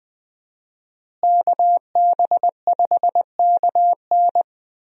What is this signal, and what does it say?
Morse code, heard as a keyed tone, about 20 wpm, 700 Hz. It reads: KB5KN
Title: USA Call Signs - 5 Characters - Repeated 2x - 20wpm